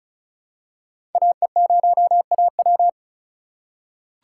Morse code, heard as a keyed tone, about 34 wpm, 700 Hz. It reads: AE0AW
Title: USA Call Signs - 5 Characters - 3x Character Spacing - 35wpm